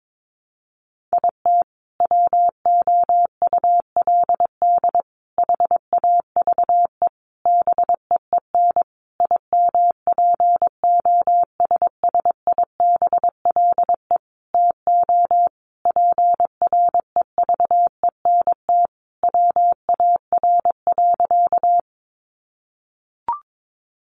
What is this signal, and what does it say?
Morse code, heard as a keyed tone, about 22 wpm, 700 Hz. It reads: IT WOULD HAVE BEEN IMPOSSIBLE TO PREVENT WAR.
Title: Sentences from Top 1100 Words 22wpm